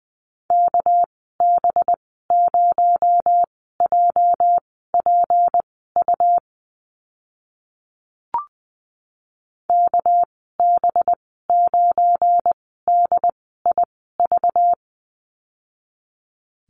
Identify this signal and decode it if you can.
Morse code, heard as a keyed tone, about 20 wpm, spaced as for 14 wpm, 700 Hz. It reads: KB0JPU KB9DIV
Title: 245 Lesson 39g - Letter z - Call Signs 20wpm